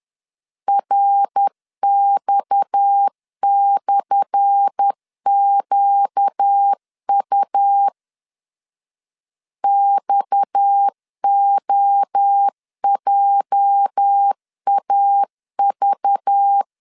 Morse code, heard as a keyed tone, about 10 wpm, 800 Hz. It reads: RX/QU XOJAV
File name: LEZIONI / 10 SIGNS / 10A SLASH 10wpm 2808